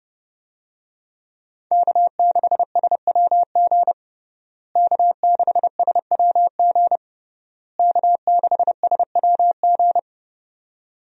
Morse code, heard as a keyed tone, about 32 wpm, 700 Hz. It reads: K6SWG K6SWG K6SWG
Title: USA Call Signs - 5 Characters - Repeated 3x - 30wpm